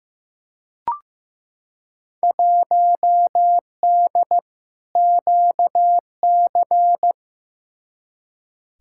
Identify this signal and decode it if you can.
Morse code, heard as a keyed tone, about 15 wpm, 700 Hz. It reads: 1D QC
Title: Field Day Exchanges - 15wpm